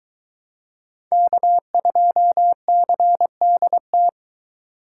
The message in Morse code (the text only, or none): K2CDT